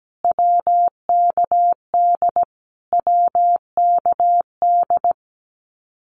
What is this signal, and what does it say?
Morse code, heard as a keyed tone, about 17 wpm, 700 Hz. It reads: WKD WKD
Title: Abbreviations Repeated 3x - 17wpm